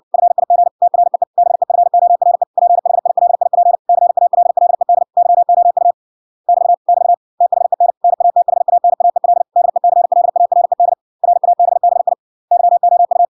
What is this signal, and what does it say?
Morse code, heard as a keyed tone, about 58 wpm, 700 Hz. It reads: FIG TREE BEFORE PHILIP CALLED YOU <BT> <BT> THEN NATHANAEL DECLARED RABBI YOU